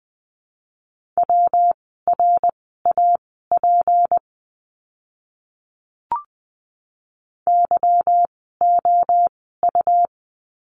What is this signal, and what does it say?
Morse code, heard as a keyed tone, about 20 wpm, spaced as for 15 wpm, 700 Hz. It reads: WRAP YOU
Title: 153 Lesson 25f - Letter p - Words 20wpm